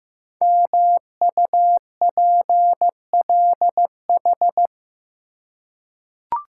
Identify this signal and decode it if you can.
Morse code, heard as a keyed tone, about 15 wpm, 700 Hz. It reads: MUPLH
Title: Code Groups - 5 Letters - 15wpm